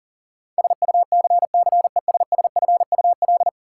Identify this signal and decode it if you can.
Morse code, heard as a keyed tone, about 40 wpm, 700 Hz. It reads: SUCCESSFUL
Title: Sentences from Top 900 Words - New Word Drill - 40wpm